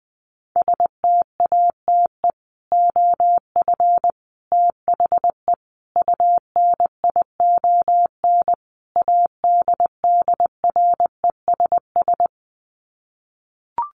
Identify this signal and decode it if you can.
Morse code, heard as a keyed tone, about 20 wpm, 700 Hz. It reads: STATE OF THE UNION ADDRESS
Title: Sets of 5 Words Encore 20wpm